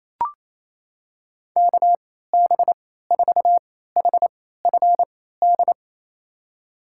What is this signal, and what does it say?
Morse code, heard as a keyed tone, about 28 wpm, spaced as for 16 wpm, 700 Hz. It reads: KB4HFD
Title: USA Call Signs - 3x Character Spacing - 28wpm